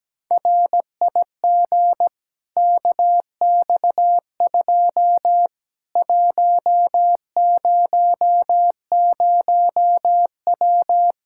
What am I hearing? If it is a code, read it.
Morse code, heard as a keyed tone, about 17 wpm, 700 Hz. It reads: RIG KX2 100W
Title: Practice QSO - Exchange 2 - Info - 17wpm